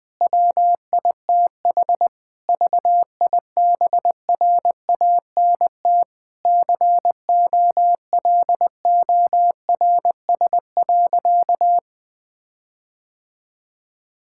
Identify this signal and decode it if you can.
Morse code, heard as a keyed tone, about 20 wpm, 700 Hz. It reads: WITH VIBRANT COLORS.
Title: Difficult Sentences with Hints - 20wpm